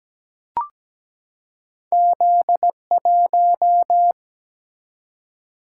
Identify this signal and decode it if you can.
Morse code, heard as a keyed tone, about 17 wpm, 700 Hz. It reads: Z1
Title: Single-Letter Single-Number 17wpm